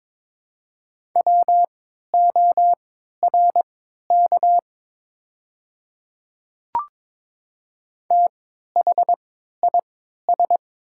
Morse code, heard as a keyed tone, about 23 wpm, spaced as for 13 wpm, 700 Hz. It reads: WORK THIS
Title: 3x Character Spacing - Top 300 Words - 22wpm